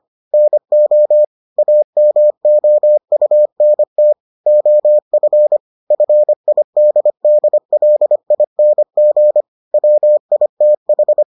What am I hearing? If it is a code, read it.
Morse code, heard as a keyed tone, about 25 wpm, 600 Hz. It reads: NO AMOUNT OF FIDDLING WITH